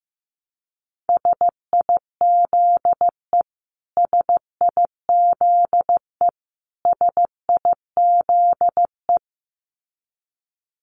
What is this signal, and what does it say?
Morse code, heard as a keyed tone, about 15 wpm, 700 Hz. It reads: SIZE SIZE SIZE
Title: Top 500 Words Repeated 15wpm